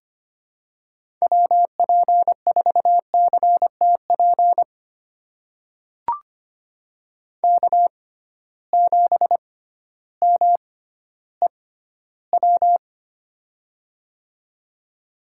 Morse code, heard as a keyed tone, about 25 wpm, spaced as for 13 wpm, 700 Hz. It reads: WP4CTP K7MEW
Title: USA Call Signs - 6x Character Spacing - 25wpm